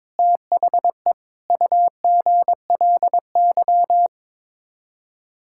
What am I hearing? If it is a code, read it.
Morse code, heard as a keyed tone, about 22 wpm, 700 Hz. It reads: THE UGLY
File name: practice / 22wpm / Sets of 7 Words - 22wpm